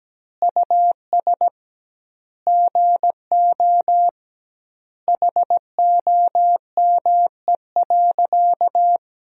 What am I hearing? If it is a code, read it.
Morse code, heard as a keyed tone, about 17 wpm, 700 Hz. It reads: US GO HOME.
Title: 2x Word-Spacing - Sentences from Top 300 Words - 17wpm